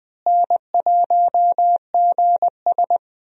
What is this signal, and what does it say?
Morse code, heard as a keyed tone, about 20 wpm, 700 Hz. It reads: N1GS
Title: USA Call Signs - 4 Characters - Repeated 2x - 20wpm